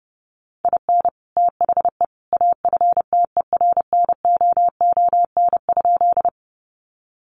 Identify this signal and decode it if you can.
Morse code, heard as a keyed tone, about 28 wpm, 700 Hz. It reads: IN THE AFTERNOON?
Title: Sentences from Top 1500 Words 30wpm